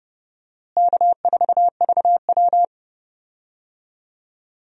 Morse code, heard as a keyed tone, about 30 wpm, 700 Hz. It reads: K4VW
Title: USA Call Signs - 4 Characters - Repeated 2x - 30wpm